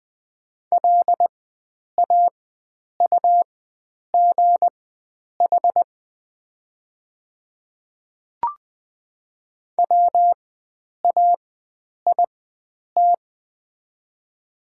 Morse code, heard as a keyed tone, about 21 wpm, spaced as for 9 wpm, 700 Hz. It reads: LAUGH WAIT
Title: 4x Character Spacing - Top 500 Words - 20wpm